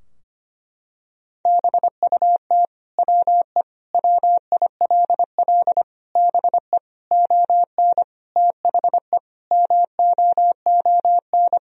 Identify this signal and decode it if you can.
Morse code, heard as a keyed tone, about 25 wpm, 700 Hz. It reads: BUT WE WILL BE ON THE MOON
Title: Sentences from Top 1000 Words - Review Part 3 - 25wpm